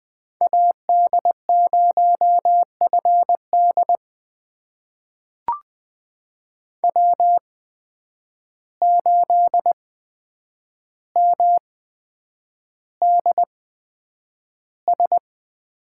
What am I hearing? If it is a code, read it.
Morse code, heard as a keyed tone, about 20 wpm, spaced as for 9 wpm, 700 Hz. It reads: AD0FD W8MDS
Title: USA Call Signs - 5 Characters - 8x Character Spacing - 20wpm